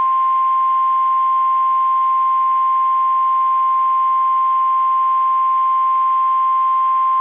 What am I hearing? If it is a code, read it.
Emergency Alert System (SAME) tones (the 1050 Hz attention tone). It is heard from 0.3 to 6.9 s.
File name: eas / KSDZKDJL / KSDZKDJL